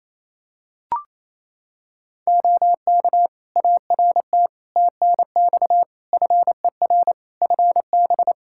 Morse code, heard as a keyed tone, about 27 wpm, 700 Hz. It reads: OK ART TNX FER FB
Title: Practice QSO - Exchange 4 - Ending - 28wpm